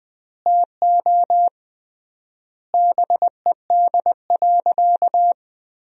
Morse code, heard as a keyed tone, about 20 wpm, 700 Hz. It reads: TO BED.
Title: 3x Word-Spacing - Sentences from Top 500 Words - 20wpm